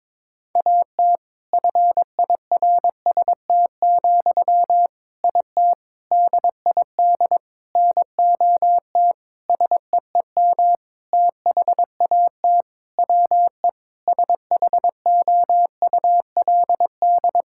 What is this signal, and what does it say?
Morse code, heard as a keyed tone, about 22 wpm, 700 Hz. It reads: AT FIRST, IT DID NOT SEEM THAT WE SHOULD
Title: Sentences from Top 1000 Words - Review Part 6 - 22wpm